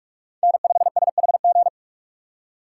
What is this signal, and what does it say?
Morse code, heard as a keyed tone, about 46 wpm, 700 Hz. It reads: N5SHG
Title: USA Call Signs - 5 Characters - 4x Character Spacing - 45wpm